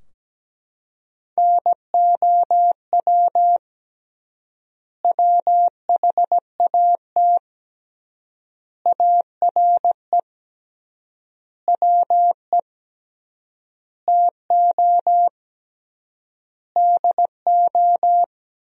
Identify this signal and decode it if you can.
Morse code, heard as a keyed tone, about 17 wpm, 700 Hz. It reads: NOW WHAT ARE WE TO DO
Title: 3x Word-Spacing - Sentences from Top 100 Words - 17wpm